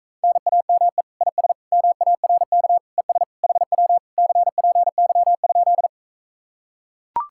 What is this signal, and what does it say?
Morse code, heard as a keyed tone, about 44 wpm, 700 Hz. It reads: NAME IS MARK ES HW CPY?